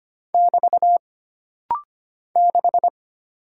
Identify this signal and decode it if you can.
Morse code, heard as a keyed tone, about 25 wpm, 700 Hz. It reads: <BT> 6
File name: Single Character (Letter:Number:Punctuation:Prosign) - Rapid-Fire - 25wpm